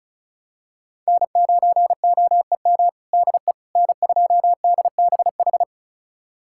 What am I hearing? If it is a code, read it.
Morse code, heard as a keyed tone, about 36 wpm, 700 Hz. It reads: N9OEM DE N2DBH
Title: USA Callsign DE Callsign - 5 Characters - 35wpm